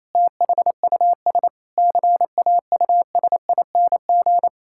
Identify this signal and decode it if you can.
Morse code, heard as a keyed tone, about 28 wpm, 700 Hz. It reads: THUS CAUSING
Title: Pangram Challenge v2 - 28wpm